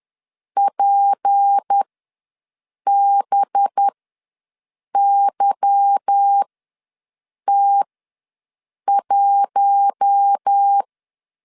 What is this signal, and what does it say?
Morse code, heard as a keyed tone, about 10 wpm, spaced as for 6 wpm, 800 Hz. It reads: PBYT1
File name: LEZIONI / 8 YJ45 / 8 YJ45 ws 10wpm 3843